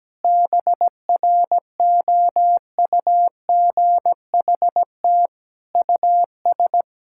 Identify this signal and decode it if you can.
Morse code, heard as a keyed tone, about 17 wpm, 700 Hz. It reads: BROUGHT US
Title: Sentences from Top 1000 Words - Review Part 6 - 17wpm